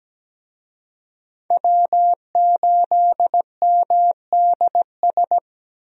Morse code, heard as a keyed tone, about 17 wpm, 700 Hz. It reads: W8MDS
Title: USA Call Signs - 5 Characters - 17wpm